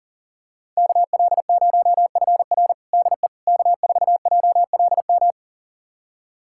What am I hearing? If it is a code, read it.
Morse code, heard as a keyed tone, about 40 wpm, 700 Hz. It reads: KL0FR DE K4JLM
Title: USA Callsign DE Callsign - 5 Characters - 40wpm